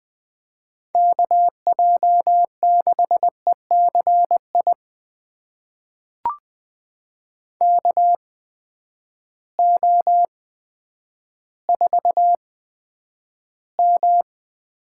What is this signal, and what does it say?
Morse code, heard as a keyed tone, about 20 wpm, spaced as for 10 wpm, 700 Hz. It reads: KJ6ECI KO4M
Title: USA Call Signs - 8x Character Spacing - 20wpm